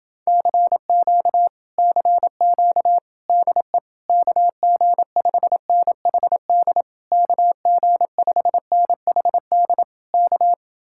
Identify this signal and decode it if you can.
Morse code, heard as a keyed tone, about 30 wpm, 700 Hz. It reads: CQ CQ DE KG5NHD KG5NHD K